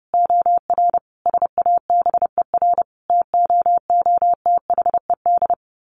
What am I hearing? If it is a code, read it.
Morse code, heard as a keyed tone, about 30 wpm, 700 Hz. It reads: OR SABER TOOTHED
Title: A Sound Of Thunder 30wpm